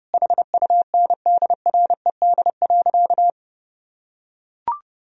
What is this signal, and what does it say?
Morse code, heard as a keyed tone, about 32 wpm, 700 Hz. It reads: HUNDRED.
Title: Sentences from Top 1000 Words - Review Part 2 - 30wpm